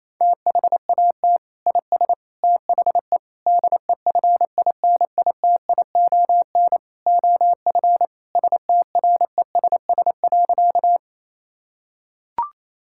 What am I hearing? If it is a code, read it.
Morse code, heard as a keyed tone, about 28 wpm, 700 Hz. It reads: THAT IS THE DEFINITION OF STRESS.
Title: Sentences from Top 1000 Words - Review Part 6 - 28wpm